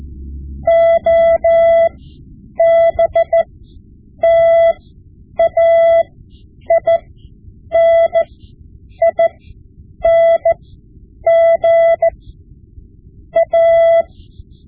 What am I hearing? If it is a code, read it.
Morse code, heard as a keyed tone, about 14 wpm, spaced as for 8 wpm, 670 Hz. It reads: OBTAINING A
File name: Morse Lessons / Morse Lessons / Tape-5B